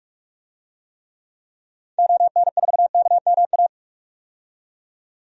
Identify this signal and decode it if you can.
Morse code, heard as a keyed tone, about 46 wpm, 700 Hz. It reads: ON4KMA
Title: DXCC-100 Call Signs - 6 Characters - 45wpm